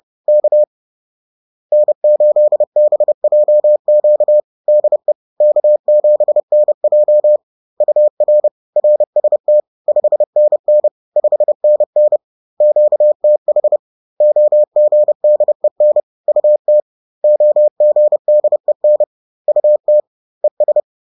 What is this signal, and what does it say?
Morse code, heard as a keyed tone, about 30 wpm, 600 Hz. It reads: K N8BJQ DE K7NJ UR RST 5NN 5NN QTH OGDEN UT OGDEN UT ES
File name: 30wpm-4